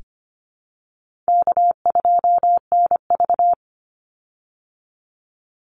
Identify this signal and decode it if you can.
Morse code, heard as a keyed tone, about 25 wpm, 700 Hz. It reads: K2NV
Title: Call Signs Repeated 2x - 25wpm